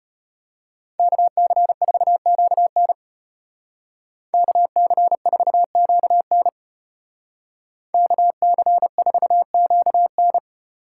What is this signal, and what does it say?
Morse code, heard as a keyed tone, about 30 wpm, 700 Hz. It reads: KC4QN KC4QN KC4QN
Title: Speed Racing - USA Call Signs 25wpm